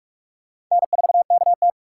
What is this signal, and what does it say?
Morse code, heard as a keyed tone, about 46 wpm, 700 Hz. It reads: N4KT
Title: USA Call Signs - 3x Character Spacing - 45wpm